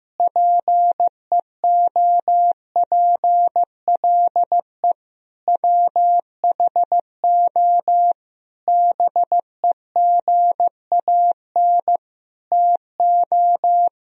Morse code, heard as a keyed tone, about 15 wpm, 700 Hz. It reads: PEOPLE WHO BEGAN TO
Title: Sentences from Top 1000 Words - Review Part 1 - 15wpm